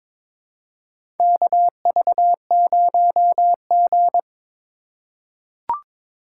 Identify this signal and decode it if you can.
Morse code, heard as a keyed tone, about 22 wpm, 700 Hz. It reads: KV0G
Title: USA Call Signs - 4x Character Spacing - 22wpm